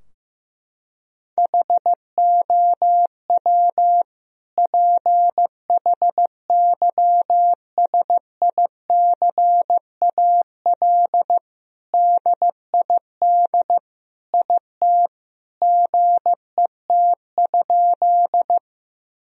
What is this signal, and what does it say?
Morse code, heard as a keyed tone, about 16 wpm, 700 Hz. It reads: HOW PHYSICAL DID IT GET?
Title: Sentences from Top 1000 Words - Review Part 4 - 15wpm